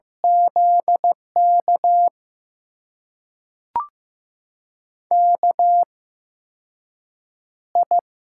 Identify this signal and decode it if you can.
Morse code, heard as a keyed tone, about 15 wpm, spaced as for 3 wpm, 700 Hz. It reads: Z KKI
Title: USA Call Signs - 6 Characters - 8x Character Spacing - 15wpm